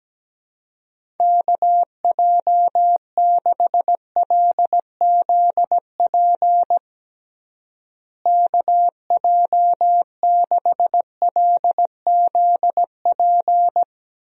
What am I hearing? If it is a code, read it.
Morse code, heard as a keyed tone, about 17 wpm, 700 Hz. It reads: KJ6LZP KJ6LZP
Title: USA Call Signs - 6 Characters - Repeated 2x - 17wpm